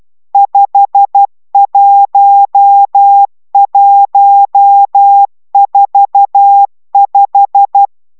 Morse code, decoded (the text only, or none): 51145